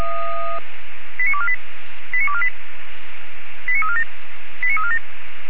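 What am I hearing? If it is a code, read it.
Five-tone selective call (ZVEI), heard as quick runs of five tones, tones of about 70 ms. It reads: 89248 89248 89347 89347
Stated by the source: ZVEI five-tone call 89360 received on an SDR.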